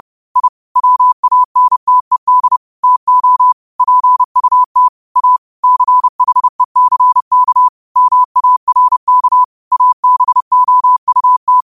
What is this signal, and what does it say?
Morse code, heard as a keyed tone, about 30 wpm, 1000 Hz. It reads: I WANTED TO PUT A CHECK MARK ABOUT